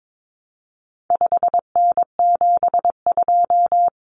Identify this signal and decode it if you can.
Morse code, heard as a keyed tone, about 22 wpm, 700 Hz. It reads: 5N72
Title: Common Cut Numbers - 4-Digit 22wpm